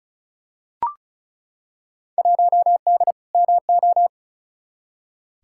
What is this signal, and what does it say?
Morse code, heard as a keyed tone, about 36 wpm, 700 Hz. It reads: 1D MO
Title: Field Day Exchanges - 35wpm